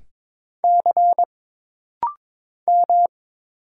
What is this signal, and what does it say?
Morse code, heard as a keyed tone, about 22 wpm, 700 Hz. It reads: C M
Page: This rapid-fire timing is geared to push your ICR (Instant Character Recognition) skills.